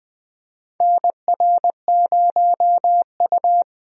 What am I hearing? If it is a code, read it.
Morse code, heard as a keyed tone, about 20 wpm, 700 Hz. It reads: NR0U
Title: USA Call Signs - 4 Characters - Repeated 3x - 20wpm